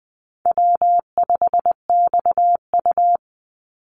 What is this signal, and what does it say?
Morse code, heard as a keyed tone, about 20 wpm, 700 Hz. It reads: W5XU
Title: Easy Call Signs Repeated 2x - 20wpm